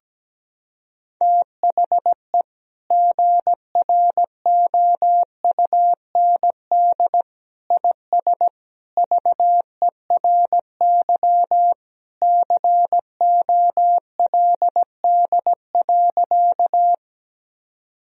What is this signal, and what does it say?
Morse code, heard as a keyed tone, about 17 wpm, 700 Hz. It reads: THE GROUND IS VERY COLD.
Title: Sentences from Top 1000 Words - Review Part 2 - 17wpm